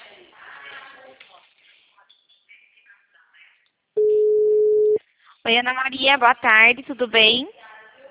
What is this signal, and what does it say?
Ringback tone (Europe, 425 Hz), with speech.